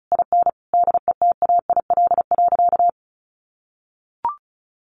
Morse code, heard as a keyed tone, about 34 wpm, 700 Hz. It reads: IN DETAIL.
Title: Sentences from Top 1500 Words 35wpm